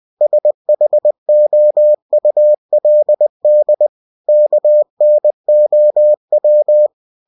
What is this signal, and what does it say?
Morse code, heard as a keyed tone, about 20 wpm, 600 Hz. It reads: SHOULD KNOW
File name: cw-books / winnie-the-pooh / winnie-20wpm-page-0039